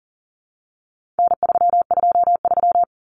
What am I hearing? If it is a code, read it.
Morse code, heard as a keyed tone, about 40 wpm, 700 Hz. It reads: N323
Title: Common Cut Numbers - 4-Digit 40wpm